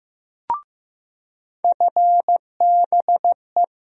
Morse code, heard as a keyed tone, about 15 wpm, 700 Hz. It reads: FBE
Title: USA Call Sign Postfixes - 3 Letters 15wpm